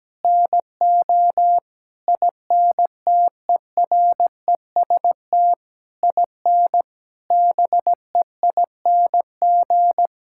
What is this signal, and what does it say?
Morse code, heard as a keyed tone, about 17 wpm, 700 Hz. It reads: NO INTEREST IN BEING
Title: Sentences from Top 1000 Words - Review Part 6 - 17wpm